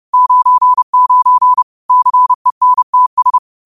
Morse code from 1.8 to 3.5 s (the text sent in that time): CENTS